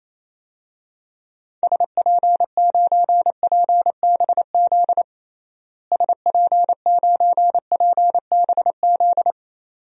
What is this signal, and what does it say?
Morse code, heard as a keyed tone, about 28 wpm, 700 Hz. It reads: SP9PBZ SP9PBZ
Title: DXCC-100 Call Signs - 6 Characters - Repeated 2x - 28wpm